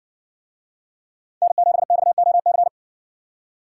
Morse code, heard as a keyed tone, about 60 wpm, 700 Hz. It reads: N7XOC